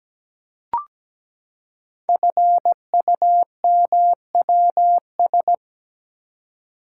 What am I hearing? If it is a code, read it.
Morse code, heard as a keyed tone, about 17 wpm, 700 Hz. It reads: FUMWS
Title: Code Groups - 5 Letters - 17wpm